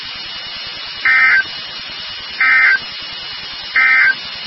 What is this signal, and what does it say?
Emergency Alert System (SAME) tones, heard as harsh buzzing bursts. It is heard from 0.9 to 4.2 s.
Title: NOAA Weather Radio WXK36